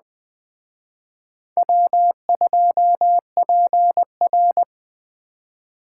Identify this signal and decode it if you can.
Morse code, heard as a keyed tone, about 20 wpm, 700 Hz. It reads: W2PR
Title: Speed Racing - USA 1x2 Call Signs 17wpm